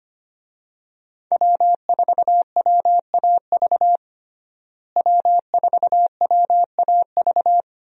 Morse code, heard as a keyed tone, about 25 wpm, 700 Hz. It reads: W4WAV W4WAV
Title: USA Call Signs - 5 Characters - Repeated 2x - 25wpm